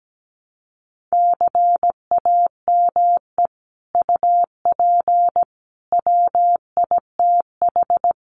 Morse code, heard as a keyed tone, about 17 wpm, 700 Hz. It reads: CAME UP WITH
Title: Sets of 5 Words Encore2 17wpm